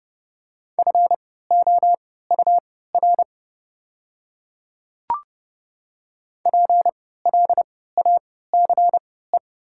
Morse code, heard as a keyed tone, about 30 wpm, spaced as for 17 wpm, 700 Hz. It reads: FOUR PLACE
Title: 3x Character Spacing - Top 300 Words - 30wpm